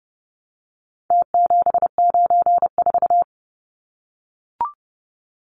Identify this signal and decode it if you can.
Morse code, heard as a keyed tone, about 30 wpm, 700 Hz. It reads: T794
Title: Numbers - Contest Style - 4-Digit 30wpm